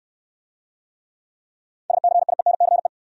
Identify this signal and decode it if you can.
Morse code, heard as a keyed tone, about 68 wpm, 700 Hz. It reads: SCIENCE